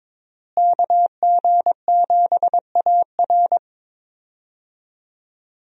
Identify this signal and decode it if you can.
Morse code, heard as a keyed tone, about 22 wpm, 700 Hz. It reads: KG7AR
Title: USA Call Signs - 5 Characters - Repeated 3x - 22wpm